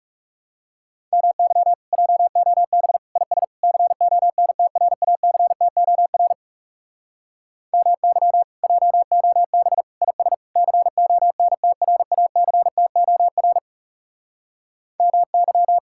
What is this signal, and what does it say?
Morse code, heard as a keyed tone, about 40 wpm, 700 Hz. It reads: MY JOB IS CONTRACTOR MY JOB IS CONTRACTOR MY